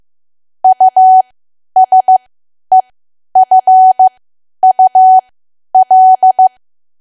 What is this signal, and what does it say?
Morse code, heard as a keyed tone, about 15 wpm, spaced as for 10 wpm, 750 Hz. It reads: USEFUL
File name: cwa / W1AW-Level2 / 10WPM